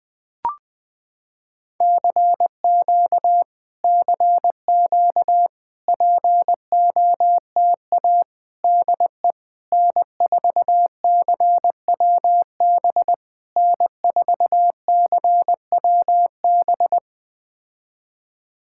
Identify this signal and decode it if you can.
Morse code, heard as a keyed tone, about 20 wpm, 700 Hz. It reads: CQ CQ POTA DE N4CWB N4CWB
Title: POTA QSO Practice - 20wpm